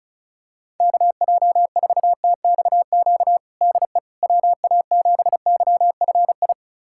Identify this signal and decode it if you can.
Morse code, heard as a keyed tone, about 34 wpm, 700 Hz. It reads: KJ4TXQ DE WA7YFI
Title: USA Callsign DE Callsign - 6 Characters - 35wpm